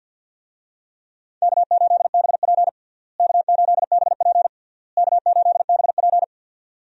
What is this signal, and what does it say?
Morse code, heard as a keyed tone, about 46 wpm, 700 Hz. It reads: K8BP K8BP K8BP
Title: USA Call Signs - 4 Characters - Repeated 3x - 50wpm